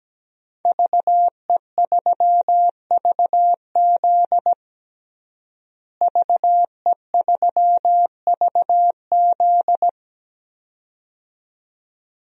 Morse code, heard as a keyed tone, about 17 wpm, 700 Hz. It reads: VE3VZ VE3VZ
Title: DXCC-100 Call Signs - 5 Characters - Repeated 2x - 17wpm